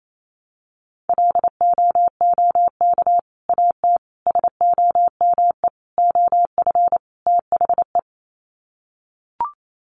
Morse code, heard as a keyed tone, about 28 wpm, 700 Hz. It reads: LOOK AT SOME OF THE
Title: Sets of 5 Words Encore 28wpm